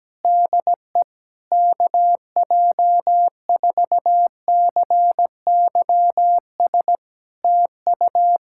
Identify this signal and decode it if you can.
Morse code, heard as a keyed tone, about 17 wpm, 700 Hz. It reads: DE KJ4CYS TU
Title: Practice QSO - Exchange 4 - Ending - 17wpm